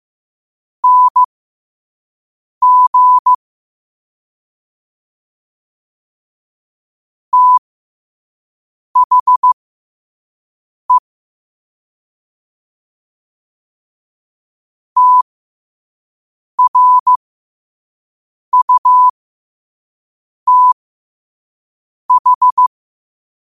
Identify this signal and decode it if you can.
Morse code, heard as a keyed tone, about 15 wpm, spaced as for 5 wpm, 1000 Hz. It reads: NG THE TRUTH
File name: qotdfiles / 5wpm